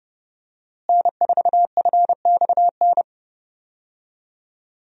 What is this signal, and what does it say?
Morse code, heard as a keyed tone, about 30 wpm, 700 Hz. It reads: N4FXN
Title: USA Call Signs - 5 Characters - Repeated 2x - 30wpm